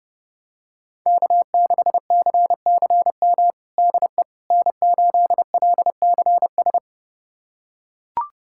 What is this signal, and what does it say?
Morse code, heard as a keyed tone, about 30 wpm, 700 Hz. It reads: K6CCM DE N8LCS
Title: USA Callsign DE Callsign - 5 Characters - 30wpm